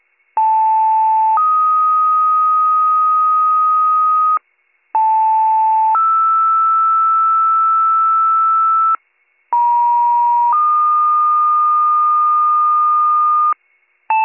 Two-tone page, heard as a one-second tone, then a three-second tone, about 880 then 1250 Hz, about 880 then 1320 Hz, about 950 then 1220 Hz. It is heard from 0.4 to 13.5 s.